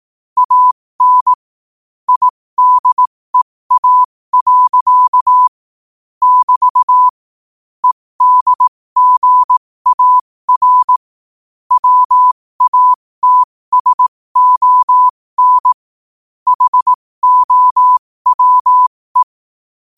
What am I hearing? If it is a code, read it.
Morse code, heard as a keyed tone, about 18 wpm, spaced as for 15 wpm, 1000 Hz. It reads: AN IDEA. <BT> EDGAR WATSON HOWE
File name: qotdfiles / 15wpm